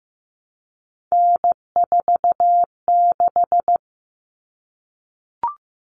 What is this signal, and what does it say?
Morse code, heard as a keyed tone, about 15 wpm, 700 Hz. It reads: N46
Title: All Cut Numbers - 3-Digit 15wpm